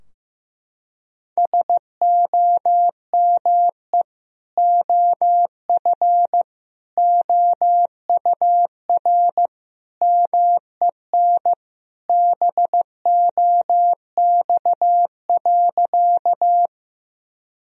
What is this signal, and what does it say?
Morse code, heard as a keyed tone, about 15 wpm, 700 Hz. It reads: SOME OF OUR MEN BOX.
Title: Sentences from Top 1000 Words - Review Part 2 - 15wpm